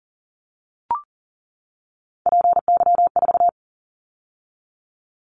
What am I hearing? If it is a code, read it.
Morse code, heard as a keyed tone, about 40 wpm, 700 Hz. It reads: PY4
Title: Callsign Prefixes for Top 100 DXCC - 40wpm